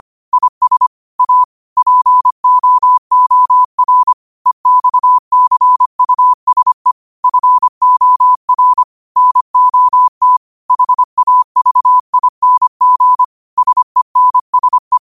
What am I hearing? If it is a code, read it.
Morse code, heard as a keyed tone, about 25 wpm, 1000 Hz. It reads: IS A POOR EXCUSE FOR NOT HAVING SENSE